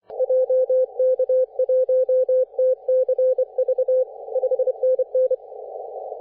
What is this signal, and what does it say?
Morse code, heard as a keyed tone, about 25 wpm, 510 Hz. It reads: JK1TCV 5NN
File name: 3D2YA 24.913MHz CW